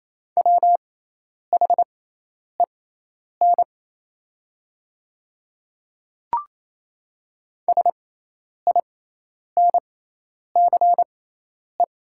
Morse code, heard as a keyed tone, about 27 wpm, spaced as for 10 wpm, 700 Hz. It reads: WHEN SINCE
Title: 6x Character Spacing - Top 500 Words - 28wpm